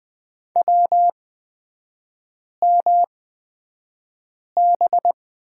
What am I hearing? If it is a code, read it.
Morse code, heard as a keyed tone, about 20 wpm, spaced as for 5 wpm, 700 Hz. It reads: WMB
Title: 250 Lesson 40e - BK - Three Characters 20wpm